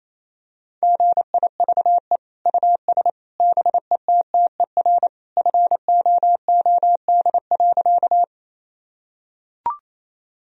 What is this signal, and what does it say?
Morse code, heard as a keyed tone, about 27 wpm, 700 Hz. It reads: GIVE US BETTER FOOD.
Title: Sentences from Top 1000 Words - Review Part 1 - 28wpm